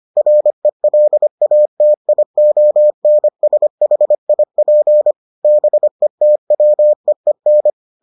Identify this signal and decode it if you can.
Morse code, heard as a keyed tone, about 25 wpm, 600 Hz. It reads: RELATIONSHIP BETWEEN